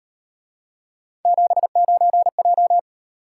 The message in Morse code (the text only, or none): 79J